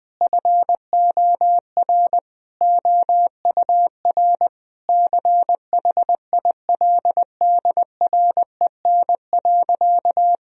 Morse code, heard as a keyed tone, about 20 wpm, 700 Hz. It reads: FOR OUR CHILDREN.
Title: Sentences from Top 1000 Words - Review Part 4 - 20wpm